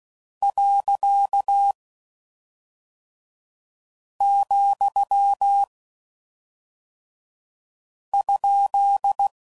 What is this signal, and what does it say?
Morse code, heard as a keyed tone, about 16 wpm, 800 Hz. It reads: . , ?